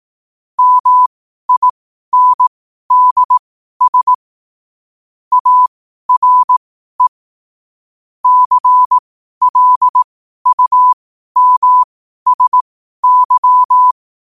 Morse code, heard as a keyed tone, about 18 wpm, spaced as for 12 wpm, 1000 Hz. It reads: MINDS ARE CLUMSY
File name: qotdfiles / 12wpm